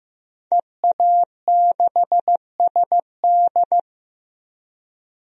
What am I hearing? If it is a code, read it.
Morse code, heard as a keyed tone, about 15 wpm, 700 Hz. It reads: EA6SD
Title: DXCC-100 Call Signs - 5 Characters - Repeated 3x - 15wpm